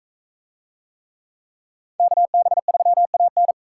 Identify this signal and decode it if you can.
Morse code, heard as a keyed tone, about 42 wpm, 700 Hz. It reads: KB3AN